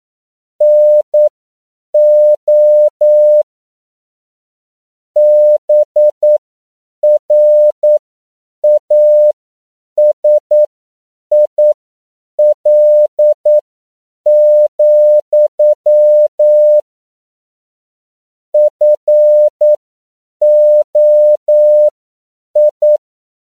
Morse code, heard as a keyed tone, about 9 wpm, spaced as for 7 wpm, 600 Hz. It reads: NO BRASIL, FOI